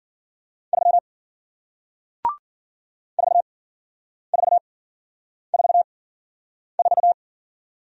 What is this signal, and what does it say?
Morse code, heard as a keyed tone, about 52 wpm, spaced as for 9 wpm, 700 Hz. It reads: 3 4444